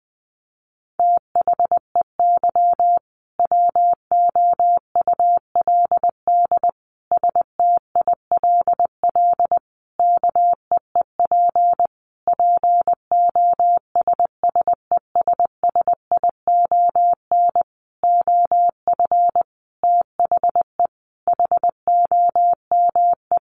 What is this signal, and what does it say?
Morse code, heard as a keyed tone, about 20 wpm, 700 Hz. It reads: THEY WOULD STILL KEEP POSSESSION OF THE HOME
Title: Sentences from Top 1400 Words 20wpm